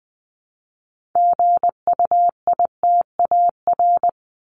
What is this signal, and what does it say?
Morse code, heard as a keyed tone, about 20 wpm, 700 Hz. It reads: GUITAR
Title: Top 1501-1750 Words 20wpm